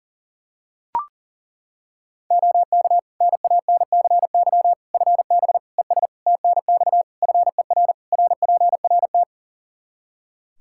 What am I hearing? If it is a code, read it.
Morse code, heard as a keyed tone, about 40 wpm, 700 Hz. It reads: OK NANCY FB ES TNX FER RPRT
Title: Practice QSO - Exchange 2 - Info - 40wpm